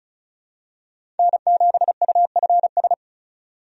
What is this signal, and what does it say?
Morse code, heard as a keyed tone, about 36 wpm, 700 Hz. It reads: N7UFS
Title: USA Call Signs - 5 Characters - 35wpm